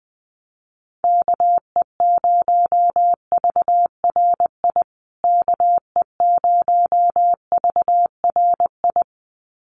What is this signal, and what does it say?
Morse code, heard as a keyed tone, about 20 wpm, 700 Hz. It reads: KE0VRI KE0VRI
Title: USA Call Signs Repeated 2x 20wpm